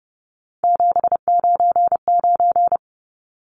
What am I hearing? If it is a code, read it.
Morse code, heard as a keyed tone, about 30 wpm, 700 Hz. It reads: 799
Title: Numbers - 3-Digit 30wpm